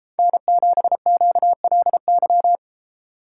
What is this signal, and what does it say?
Morse code, heard as a keyed tone, about 32 wpm, 700 Hz. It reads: N7QLY